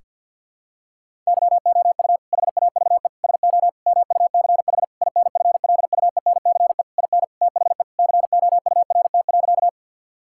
Morse code, heard as a keyed tone, about 50 wpm, 700 Hz. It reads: YOU HAVE SO MUCH INFLUENCE IN THE COURT.
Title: Sentences from Top 1600 Words 50wpm